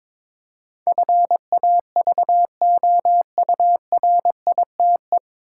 Morse code, heard as a keyed tone, about 22 wpm, 700 Hz. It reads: FAVOURITE
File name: 9 Letter Words - First 3 Letters Given - 22wpm